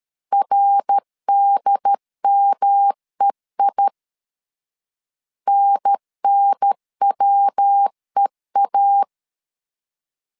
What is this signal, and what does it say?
Morse code, heard as a keyed tone, about 12 wpm, 800 Hz. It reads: RDMEI NNWEA
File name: CW / DET / 1 3 1500C S B 12wpm 2828